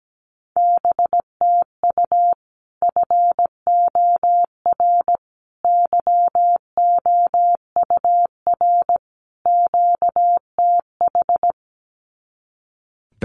Morse code, heard as a keyed tone, about 17 wpm, 700 Hz. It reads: BTU FOR YOUR QTH
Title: Common QSO Phrases 17wpm